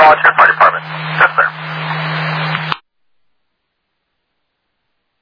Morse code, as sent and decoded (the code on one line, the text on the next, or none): none